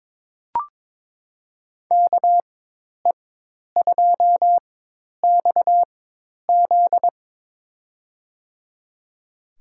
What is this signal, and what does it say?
Morse code, heard as a keyed tone, about 22 wpm, spaced as for 10 wpm, 700 Hz. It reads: KE2XZ
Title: USA Call Signs - 5 Characters - 4x Character Spacing - 22wpm